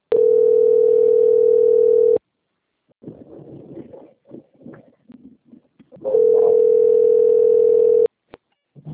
Ringback tone (North America), 440 + 480 Hz.
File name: rg-1600-unknown- - - .8507